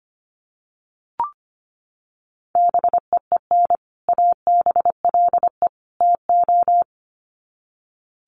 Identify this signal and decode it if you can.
Morse code, heard as a keyed tone, about 25 wpm, 700 Hz. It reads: BEEN ABLE TO
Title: Sets of 3 Words 25wpm